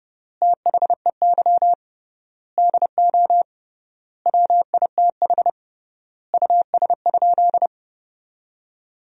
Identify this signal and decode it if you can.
Morse code, heard as a keyed tone, about 30 wpm, 700 Hz. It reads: THEY DO WITH US?
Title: 3x Word-Spacing - Sentences from Top 200 Words - 30wpm